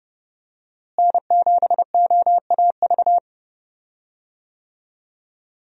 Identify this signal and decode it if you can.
Morse code, heard as a keyed tone, about 32 wpm, 700 Hz. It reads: N7OAV
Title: USA Call Signs - 5 Characters - Repeated 2x - 30wpm